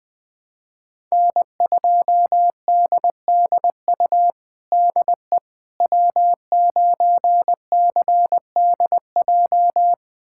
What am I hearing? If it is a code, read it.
Morse code, heard as a keyed tone, about 20 wpm, 700 Hz. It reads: N2DDU DE W9CDJ
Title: USA Callsign DE Callsign - 5 Characters - 20wpm